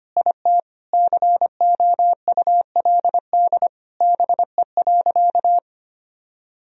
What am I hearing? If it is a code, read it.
Morse code, heard as a keyed tone, about 25 wpm, 700 Hz. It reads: IT COULD BE.
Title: Sentences from Top 1000 Words - Review Part 3 - 25wpm